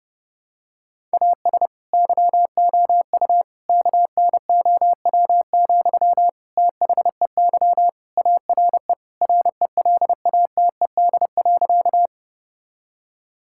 Morse code, heard as a keyed tone, about 30 wpm, 700 Hz. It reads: AS YOU KNOW, THEY ARE RELATED.
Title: Sentences from Top 1000 Words - Review Part 5 - 30wpm